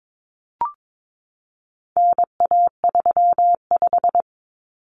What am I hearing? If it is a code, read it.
Morse code, heard as a keyed tone, about 22 wpm, 700 Hz. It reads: NA35
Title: Common Cut Numbers - 4-Digit 22wpm